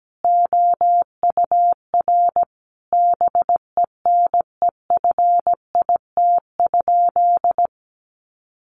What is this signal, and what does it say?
Morse code, heard as a keyed tone, about 17 wpm, 700 Hz. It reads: OUR BENEFIT?
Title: Sentences from Top 1500 Words 17wpm